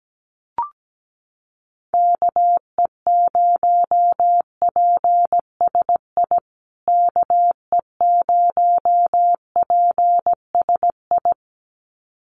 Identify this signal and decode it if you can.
Morse code, heard as a keyed tone, about 17 wpm, 700 Hz. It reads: KE0PSI KE0PSI
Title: USA Call Signs Repeated 2x 17wpm